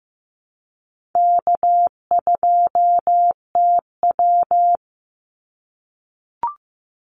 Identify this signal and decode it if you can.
Morse code, heard as a keyed tone, about 15 wpm, 700 Hz. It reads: K2TW
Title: Easy Call Signs - 15wpm